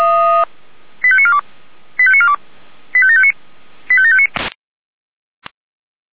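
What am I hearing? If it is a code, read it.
Five-tone selective call (ZVEI), heard as quick runs of five tones, tones of about 70 ms. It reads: 85942 85942 85759 85759
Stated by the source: Five-tone ZVEI selective call received on an SDR.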